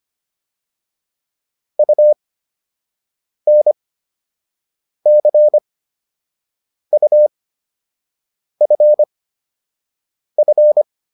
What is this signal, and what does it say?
Morse code, heard as a keyed tone, about 25 wpm, spaced as for 6 wpm, 600 Hz. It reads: UNCUFF